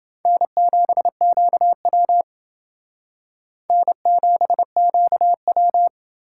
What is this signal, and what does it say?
Morse code, heard as a keyed tone, about 30 wpm, 700 Hz. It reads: N7QW N7QW